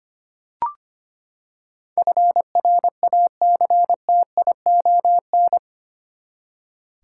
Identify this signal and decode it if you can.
Morse code, heard as a keyed tone, about 25 wpm, 700 Hz. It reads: FRACTION
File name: Eight Letter Words - 25wpm